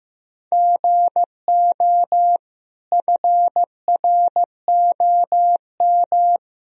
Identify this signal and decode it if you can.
Morse code, heard as a keyed tone, about 15 wpm, 700 Hz. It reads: GO FROM
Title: Sentences from Top 1000 Words - Review Part 2 - 15wpm